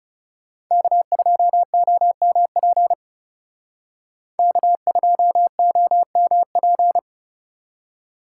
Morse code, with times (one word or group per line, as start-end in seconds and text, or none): 0.7-2.9 K2OMP
4.4-7.0 K2OMP